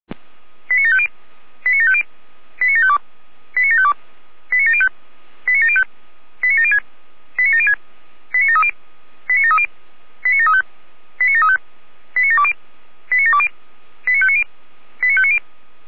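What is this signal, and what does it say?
Five-tone selective call (ZVEI), heard as quick runs of five tones, tones of about 70 ms. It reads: 89750 89750 89752 89752 89705 89705 89706 89706 89730 89730 89735 89735 89720 89720 89590 89590
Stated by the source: ZVEI five-tone selective call 89701 received by SDR.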